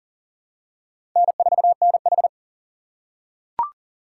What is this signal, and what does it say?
Morse code, heard as a keyed tone, about 40 wpm, 700 Hz. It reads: N4NH
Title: USA Call Signs - 4 Characters - 4x Character Spacing - 40wpm